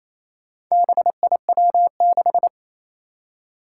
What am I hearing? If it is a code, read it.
Morse code, heard as a keyed tone, about 29 wpm, 700 Hz. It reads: BIW6
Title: Four Challenging Letter-Number Combinations 28wpm